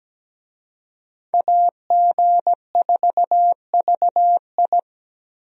Morse code, heard as a keyed tone, about 17 wpm, 700 Hz. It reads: AG4VI
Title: USA Call Signs - 5 Characters - 17wpm